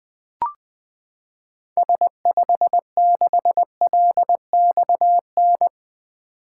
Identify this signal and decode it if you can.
Morse code, heard as a keyed tone, about 20 wpm, 700 Hz. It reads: S56LXN
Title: 2023 SOTA Activator Call Signs - 20wpm